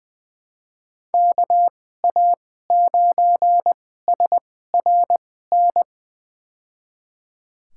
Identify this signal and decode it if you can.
Morse code, heard as a keyed tone, about 20 wpm, spaced as for 14 wpm, 700 Hz. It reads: KA9SRN
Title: USA Call Signs - 6 Characters - 2x Character Spacing - 20wpm